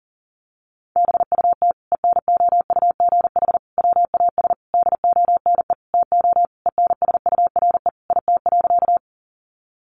Morse code, heard as a keyed tone, about 37 wpm, 700 Hz. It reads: BUT ENOUGH WAS DONE TO ENSURE IT.
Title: Sentences from Top 1100 Words 40wpm